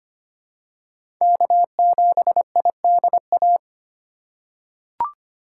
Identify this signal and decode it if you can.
Morse code, heard as a keyed tone, about 25 wpm, 700 Hz. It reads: K7IDA
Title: USA Call Signs - 3x Character Spacing - 25wpm